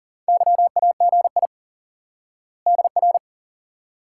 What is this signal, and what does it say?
Morse code, heard as a keyed tone, about 40 wpm, 700 Hz. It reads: YAGI DR
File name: IQE Course 3 - Lesson 89 - YAGI - Two Elements - 40wpm